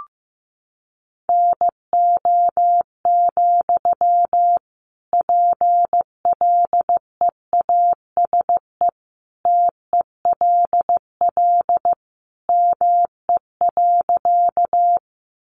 Morse code, heard as a keyed tone, about 15 wpm, 700 Hz. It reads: NO, PLEASE TELL ME.
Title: Sentences from Top 1500 Words 15wpm